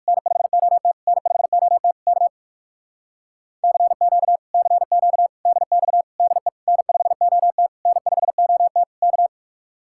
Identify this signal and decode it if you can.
Morse code, heard as a keyed tone, about 57 wpm, 700 Hz. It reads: N5OT N5OT K CQ CQ DX DE N5OT N5OT K